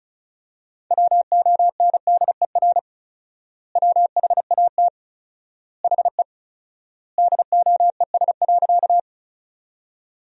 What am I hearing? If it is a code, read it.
Morse code, heard as a keyed tone, about 34 wpm, 700 Hz. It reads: WONDER WHAT HE DOES.
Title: 4x Word-Spacing - Sentences from Top 500 Words - 35wpm